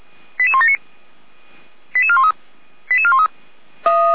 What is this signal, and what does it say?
Five-tone selective call (ZVEI), heard as quick runs of five tones, tones of about 70 ms. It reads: 88179 88413 88413
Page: ZVEI